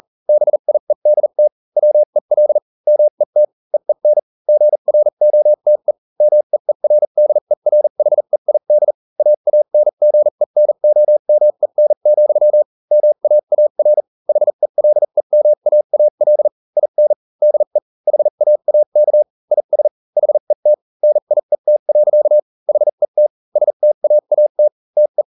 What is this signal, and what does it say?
Morse code, heard as a keyed tone, about 40 wpm, spaced as for 35 wpm, 600 Hz. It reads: BIEDT WEL MET EEN GROTE MEERDERHEID AANGENOMEN, MAAR HELEMAAL IN DE HAAK IS HET NIET. HET STAAT TE